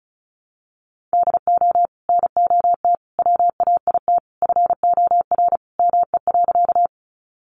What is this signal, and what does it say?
Morse code, heard as a keyed tone, about 34 wpm, 700 Hz. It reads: DO NOT WAIT FOR ME.
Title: Sentences from Top 600 Words 35wpm